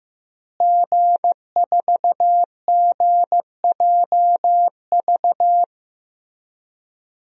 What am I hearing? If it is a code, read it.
Morse code, heard as a keyed tone, about 15 wpm, 700 Hz. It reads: G4GJV
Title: DXCC-100 Call Signs - 5 Characters - Repeated 3x - 15wpm